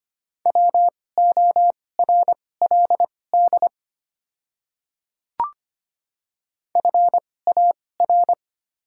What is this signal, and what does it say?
Morse code, heard as a keyed tone, about 25 wpm, spaced as for 18 wpm, 700 Hz. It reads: WORLD FAR
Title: 2x Character Spacing - Top 500 Words - 25wpm